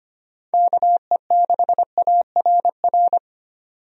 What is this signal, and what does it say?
Morse code, heard as a keyed tone, about 25 wpm, 700 Hz. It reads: KE6ARR
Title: USA Call Signs - 6 Characters - 25wpm